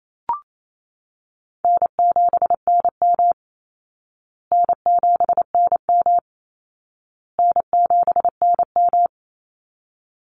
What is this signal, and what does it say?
Morse code, heard as a keyed tone, about 28 wpm, 700 Hz. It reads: N7NM N7NM N7NM
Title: Call Signs Repeated 2x - 28wpm